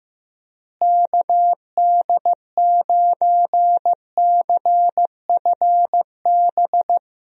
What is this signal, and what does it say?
Morse code, heard as a keyed tone, about 15 wpm, 700 Hz. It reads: KD9CFB
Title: USA Call Signs - 6 Characters - Repeated 2x - 15wpm